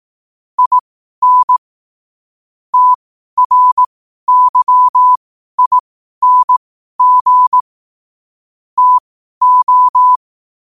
Morse code, heard as a keyed tone, about 18 wpm, spaced as for 12 wpm, 1000 Hz. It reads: IN TRYING TO
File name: qotdfiles / 12wpm